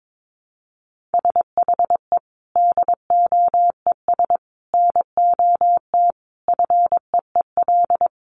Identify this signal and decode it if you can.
Morse code, heard as a keyed tone, about 22 wpm, 700 Hz. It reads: SHE DOES NOT FEEL
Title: Additional Easy Phrases 22wpm